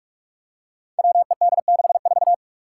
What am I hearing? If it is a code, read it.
Morse code, heard as a keyed tone, about 44 wpm, 700 Hz. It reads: WED64
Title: Five Challenging Letter-Number Combinations 45wpm